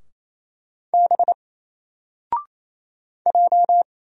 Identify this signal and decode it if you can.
Morse code, heard as a keyed tone, about 28 wpm, 700 Hz. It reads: B J
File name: Single-Letter-Number-28wpm